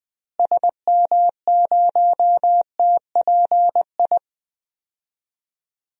Morse code, heard as a keyed tone, about 21 wpm, 700 Hz. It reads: SM0TPI
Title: DXCC-100 Call Signs - 6 Characters - Repeated 2x - 20wpm